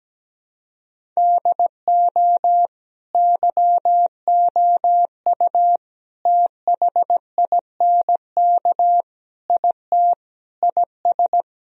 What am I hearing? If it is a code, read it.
Morse code, heard as a keyed tone, about 17 wpm, 700 Hz. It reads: DO YOU THINK IT IS
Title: Sentences from Top 1000 Words - Review Part 1 - 17wpm